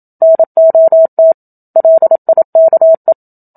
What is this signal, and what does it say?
Morse code, heard as a keyed tone, about 27 wpm, 650 Hz. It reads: NOT LIKE
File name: Book Two / 01-Chapter 1 - 27 WPM